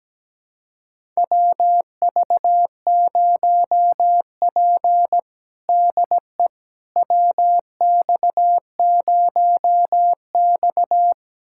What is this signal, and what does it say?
Morse code, heard as a keyed tone, about 17 wpm, 700 Hz. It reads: WV0P DE WX0X
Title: USA Callsign DE Callsign - 4 Characters - 17wpm